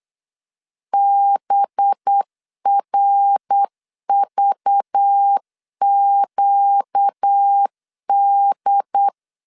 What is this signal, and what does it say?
Morse code, heard as a keyed tone, about 8 wpm, 800 Hz. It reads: BRVQD
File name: 8wpm